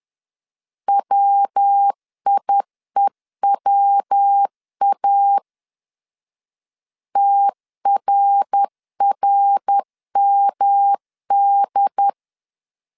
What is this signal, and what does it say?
Morse code, heard as a keyed tone, about 10 wpm, 800 Hz. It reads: WIEWA TRRMD
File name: CW / DET / 1 3 1500C S B 10wpm 3335